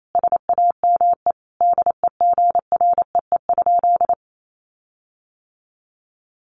Morse code, heard as a keyed tone, about 28 wpm, 700 Hz. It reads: SAME DEGREE?
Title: Sentences from Top 1500 Words 28wpm